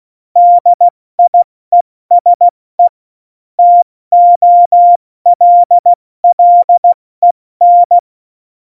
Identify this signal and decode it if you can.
Morse code, heard as a keyed tone, about 16 wpm, 700 Hz. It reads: DIESE TOLLEN
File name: Morsekurs / Witze / Witz 26. Januar 2021, 70 F80